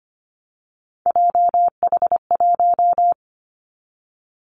Morse code, heard as a keyed tone, about 25 wpm, 700 Hz. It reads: JH1
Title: Callsign Prefixes for Top 25 DXCC - 25wpm